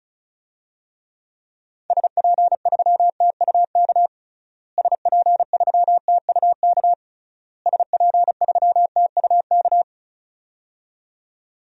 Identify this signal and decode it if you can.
Morse code, heard as a keyed tone, about 36 wpm, 700 Hz. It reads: SP3TUK SP3TUK SP3TUK
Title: DXCC-100 Call Signs - 6 Characters - Repeated 3x - 35wpm